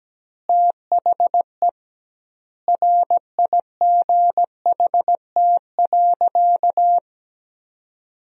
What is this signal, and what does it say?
Morse code, heard as a keyed tone, about 17 wpm, 700 Hz. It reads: THE RIGHT.
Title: 2x Word-Spacing - Sentences from Top 300 Words - 17wpm